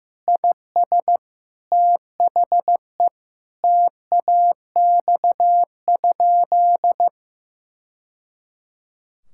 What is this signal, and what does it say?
Morse code, heard as a keyed tone, about 15 wpm, 700 Hz. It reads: IS THE TAX?
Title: Sentences from Top 1000 Words - Review Part 5 - 15wpm